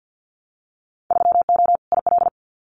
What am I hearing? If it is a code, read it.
Morse code, heard as a keyed tone, about 50 wpm, 700 Hz. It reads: 3O IL
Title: Winter Field Day Exchanges 50wpm